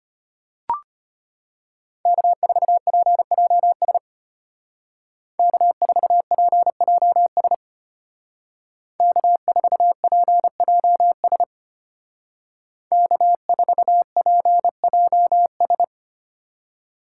Morse code, heard as a keyed tone, about 33 wpm, 700 Hz. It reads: K4PJS K4PJS K4PJS K4PJS